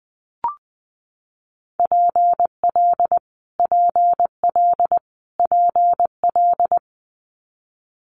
Morse code, heard as a keyed tone, about 20 wpm, 700 Hz. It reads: PL PL PL
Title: Multi-letter Phonemes Repeated 3x 20wpm